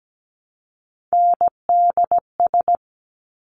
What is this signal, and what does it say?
Morse code, heard as a keyed tone, about 17 wpm, 700 Hz. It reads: NDS
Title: Top Three Letter Combinations in Words 17wpm